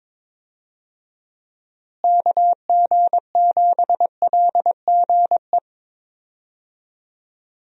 Morse code, heard as a keyed tone, about 22 wpm, 700 Hz. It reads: KG7LGE
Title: USA Call Signs - 6 Characters - Repeated 3x - 22wpm